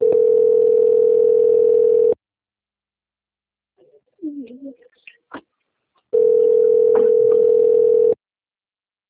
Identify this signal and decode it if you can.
Ringback tone (North America), 440 + 480 Hz.
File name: out- - - - - .18646